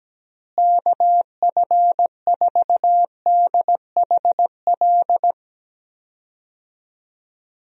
Morse code, heard as a keyed tone, about 17 wpm, 700 Hz. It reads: KF4DHL
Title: USA Call Signs - 6 Characters - Repeated 2x - 17wpm